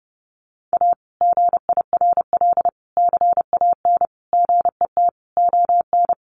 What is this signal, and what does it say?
Morse code, heard as a keyed tone, about 30 wpm, 700 Hz. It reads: A GIRL CAN GET ON
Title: Sentences from Top 1400 Words 30wpm